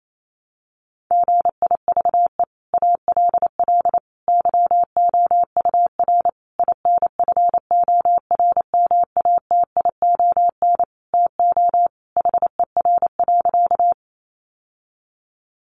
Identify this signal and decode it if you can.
Morse code, heard as a keyed tone, about 28 wpm, 700 Hz. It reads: GIVE ALL YOUR INFORMATION TO HER.
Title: Sentences from Top 600 Words 28wpm